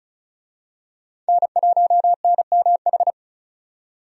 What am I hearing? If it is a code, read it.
Morse code, heard as a keyed tone, about 36 wpm, 700 Hz. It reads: N1NMH
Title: USA Call Signs - 5 Characters - 35wpm